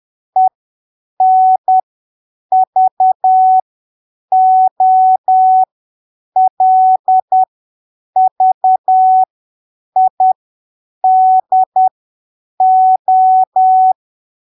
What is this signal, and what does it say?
Morse code, heard as a keyed tone, about 10 wpm, spaced as for 7 wpm, 750 Hz. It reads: ENVOLVIDO
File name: arquivos / texto / texto 11 10ppm